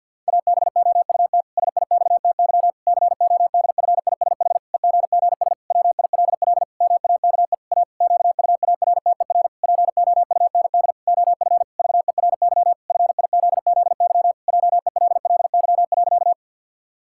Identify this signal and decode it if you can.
Morse code, heard as a keyed tone, about 52 wpm, 700 Hz. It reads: ABOUT SIXTY CODFISH EGGS WILL MAKE A QUARTER POUND OF VERY FIZZY JELLY.
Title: Pangram Challenge v2 - 50wpm